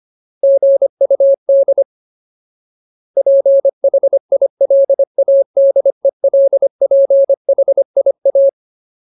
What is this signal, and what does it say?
Morse code, heard as a keyed tone, about 25 wpm, 550 Hz. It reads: GUD PHILADELPHIA
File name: Ultimate Word List / 25WPM Ultimate